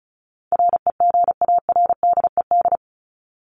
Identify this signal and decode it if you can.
Morse code, heard as a keyed tone, about 34 wpm, 700 Hz. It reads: REGARDED
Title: ED Suffix - Top 100 Words with Base Words 35wpm